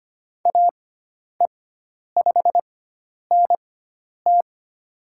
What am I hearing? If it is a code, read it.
Morse code, heard as a keyed tone, about 25 wpm, spaced as for 10 wpm, 700 Hz. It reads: AE5NT
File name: Lesson 16g - Letter c - Call Signs 25wpm